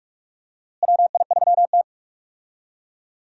Morse code, heard as a keyed tone, about 46 wpm, 700 Hz. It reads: WI3T
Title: USA Call Signs - 4 Characters - Repeated 3x - 45wpm